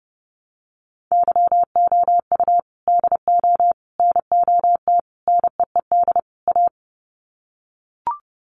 Morse code, heard as a keyed tone, about 30 wpm, 700 Hz. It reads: YOU DO NOT NEED A
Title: Sets of 5 Words Encore 30wpm